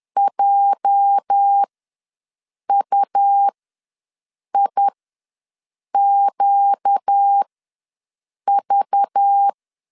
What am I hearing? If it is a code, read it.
Morse code, heard as a keyed tone, about 10 wpm, spaced as for 6 wpm, 800 Hz. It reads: JUIQV
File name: LEZIONI / 9 286 ws 10wpm 4032